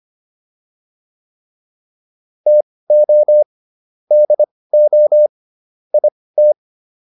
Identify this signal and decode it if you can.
Morse code, heard as a keyed tone, about 25 wpm, spaced as for 18 wpm, 600 Hz. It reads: TO DO IT